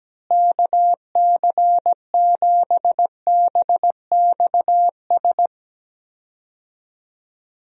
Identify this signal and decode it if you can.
Morse code, heard as a keyed tone, about 17 wpm, 700 Hz. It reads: KC7BXS
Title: USA Call Signs - 6 Characters - Repeated 3x - 17wpm